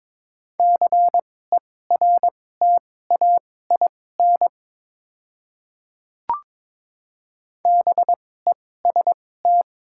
Morse code, heard as a keyed tone, about 22 wpm, spaced as for 16 wpm, 700 Hz. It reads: CERTAIN BEST
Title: 2x Character Spacing - Top 500 Words - 22wpm